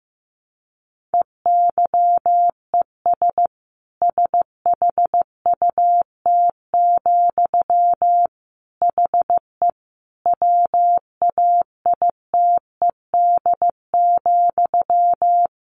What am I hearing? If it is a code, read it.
Morse code, heard as a keyed tone, about 14 wpm, 700 Hz. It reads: EYES SHUT, HE WAITED,
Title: A Sound Of Thunder 15wpm